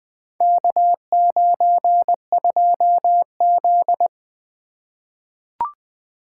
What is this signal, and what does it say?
Morse code, heard as a keyed tone, about 20 wpm, 700 Hz. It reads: K92Z
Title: Four Letter-Number Combinations - 20wpm